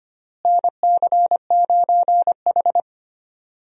Morse code, heard as a keyed tone, about 25 wpm, 700 Hz. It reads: NC9H
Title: USA Call Signs - 4 Characters - Repeated 3x - 25wpm